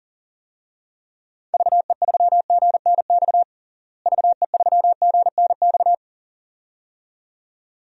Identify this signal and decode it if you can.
Morse code, heard as a keyed tone, about 40 wpm, 700 Hz. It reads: VE3GNX VE3GNX
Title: DXCC-100 Call Signs - 6 Characters - Repeated 2x - 40wpm